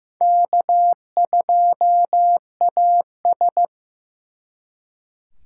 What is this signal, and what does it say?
Morse code, heard as a keyed tone, about 18 wpm, 700 Hz. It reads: K2AS
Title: Speed Racing - USA 1x2 Call Signs 15wpm